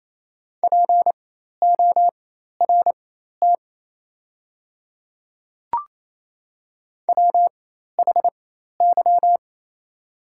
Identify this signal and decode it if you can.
Morse code, heard as a keyed tone, about 27 wpm, spaced as for 13 wpm, 700 Hz. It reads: PORT WHY
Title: 4x Character Spacing - Top 200 Words - 28wpm